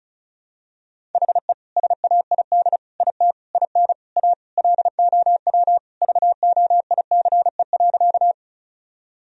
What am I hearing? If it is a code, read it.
Morse code, heard as a keyed tone, about 36 wpm, 700 Hz. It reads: HE SAID IT IN A LOW VOICE.
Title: Sentences from Top 1000 Words - Review Part 1 - 35wpm